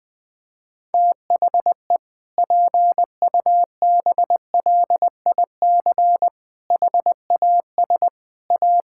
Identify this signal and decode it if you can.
Morse code, heard as a keyed tone, about 20 wpm, 700 Hz. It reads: THE PUBLIC HAS A
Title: Sentences with 7-Word Phrases - Spoken Beginning - 20wpm